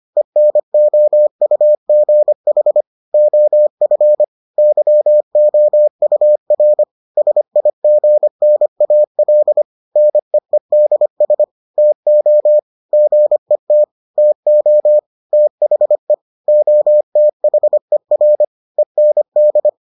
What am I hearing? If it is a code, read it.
Morse code, heard as a keyed tone, about 25 wpm, 600 Hz. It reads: ENOUGH OF YOUR SIGNAL NEEDS TO GET TO THE OTHER END